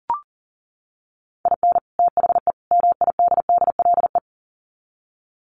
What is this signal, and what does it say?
Morse code, heard as a keyed tone, about 40 wpm, 700 Hz. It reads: IN THE MIDDLE
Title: Sets of 3 Words 40wpm